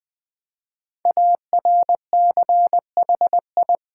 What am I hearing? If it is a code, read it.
Morse code, heard as a keyed tone, about 20 wpm, 700 Hz. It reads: ARCHI
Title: Build A Word - 5 Letters At A Time - 12-Letter Words - 20wpm